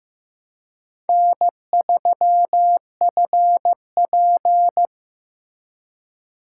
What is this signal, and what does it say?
Morse code, heard as a keyed tone, about 18 wpm, 700 Hz. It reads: N3FP
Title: Speed Racing - USA 1x2 Call Signs 15wpm